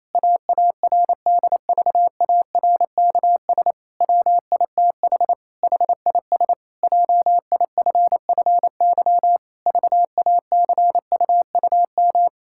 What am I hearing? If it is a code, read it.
Morse code, heard as a keyed tone, about 28 wpm, 700 Hz. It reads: AARDVARKS WITH HIS JIFFY VACUUM
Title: Pangram Challenge v2 - 28wpm